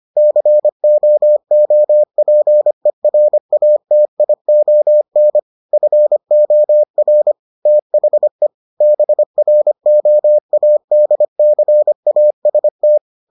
Morse code, heard as a keyed tone, about 25 wpm, 600 Hz. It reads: COOPERATION FOR THE BROADCAST